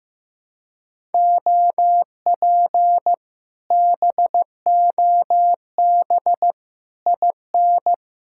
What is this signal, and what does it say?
Morse code, heard as a keyed tone, about 15 wpm, 700 Hz. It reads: OP BOB IN
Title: Speed-Racing Common QSO Phrases Encore - 15wpm